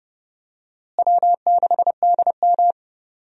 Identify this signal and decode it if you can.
Morse code, heard as a keyed tone, about 30 wpm, 700 Hz. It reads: W6DM
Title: USA Call Signs - 4 Characters - Repeated 3x - 30wpm